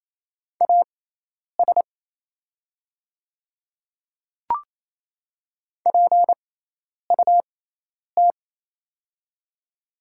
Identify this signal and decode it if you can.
Morse code, heard as a keyed tone, about 28 wpm, spaced as for 10 wpm, 700 Hz. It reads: AS PUT
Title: 6x Character Spacing - Top 200 Words - 28wpm